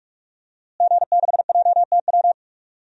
Morse code, heard as a keyed tone, about 47 wpm, 700 Hz. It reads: G6JTW
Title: DXCC-100 Call Signs - 5 Characters - Repeated 3x - 45wpm